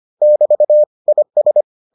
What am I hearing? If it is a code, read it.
Morse code, heard as a keyed tone, about 25 wpm, 600 Hz. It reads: <BT> IS